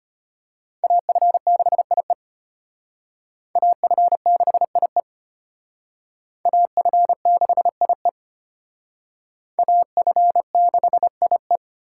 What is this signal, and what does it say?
Morse code, heard as a keyed tone, about 32 wpm, 700 Hz. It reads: AF6IE AF6IE AF6IE AF6IE